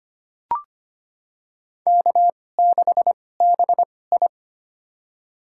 Morse code, heard as a keyed tone, about 25 wpm, spaced as for 18 wpm, 700 Hz. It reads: K6BI
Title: USA Call Signs - 4 Characters - 2x Character Spacing - 25wpm